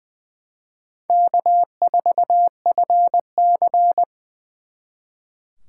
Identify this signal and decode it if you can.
Morse code, heard as a keyed tone, about 24 wpm, 700 Hz. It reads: K4FC
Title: Speed Racing - USA 1x2 Call Signs 20wpm